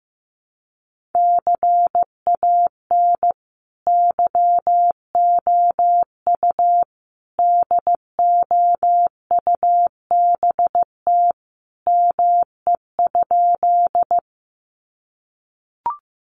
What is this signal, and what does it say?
Morse code, heard as a keyed tone, about 15 wpm, 700 Hz. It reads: CAN YOU DOUBT ME?
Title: Sentences from Top 1500 Words 15wpm